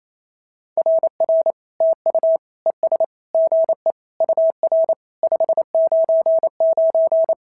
Morse code, heard as a keyed tone, about 28 wpm, 690 Hz. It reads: RR TU ES GE UR 599
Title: SOTA QSO Practice 28wpm